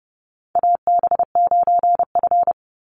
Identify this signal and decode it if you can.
Morse code, heard as a keyed tone, about 30 wpm, 700 Hz. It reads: AB9F
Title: USA Call Signs 30wpm